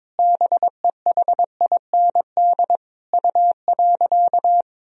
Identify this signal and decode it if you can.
Morse code, heard as a keyed tone, about 22 wpm, 700 Hz. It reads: BEHIND U.
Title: Sentences from Top 500 Words and Abbreviations 22wpm